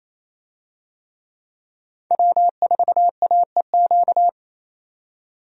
Morse code, heard as a keyed tone, about 28 wpm, 700 Hz. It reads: W4AEQ
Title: USA Call Signs - 5 Characters - 28wpm